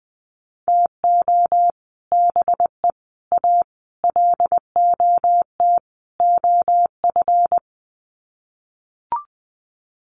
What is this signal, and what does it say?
Morse code, heard as a keyed tone, about 20 wpm, 700 Hz. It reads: TO BE A LOT OF
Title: Sets of 5 Words Encore 20wpm